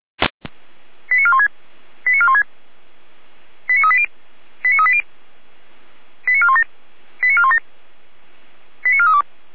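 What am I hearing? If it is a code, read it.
Five-tone selective call (ZVEI), heard as quick runs of five tones, tones of about 70 ms. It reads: 89416 89416 89380 89380 89417 89417 89432
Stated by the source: ZVEI five-tone call received on SDR.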